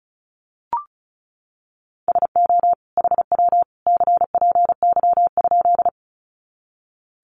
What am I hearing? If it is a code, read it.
Morse code, heard as a keyed tone, about 34 wpm, 700 Hz. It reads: SO HW CPY?
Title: Common QSO Phrases 35wpm